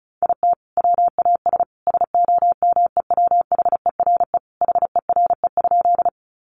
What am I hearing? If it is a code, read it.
Morse code, heard as a keyed tone, about 34 wpm, 700 Hz. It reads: IT WAS SOMEWHERE HERE?
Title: Sentences from Top 1400 Words 35wpm